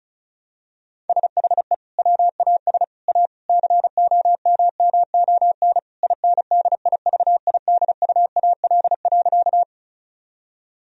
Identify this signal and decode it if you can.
Morse code, heard as a keyed tone, about 36 wpm, 700 Hz. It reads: SHE WAS A COMMON INDIVIDUAL.
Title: Sentences from Top 1000 Words - Review Part 4 - 35wpm